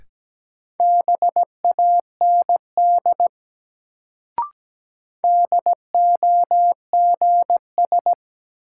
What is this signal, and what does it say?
Morse code, heard as a keyed tone, about 17 wpm, 700 Hz. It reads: BAND DOGS
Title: Sentences from Top 1600 Words - New Word Drill Alt - 17wpm